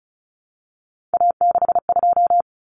Morse code, heard as a keyed tone, about 34 wpm, 700 Hz. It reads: A62
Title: Common Cut Numbers - 3-Digit 35wpm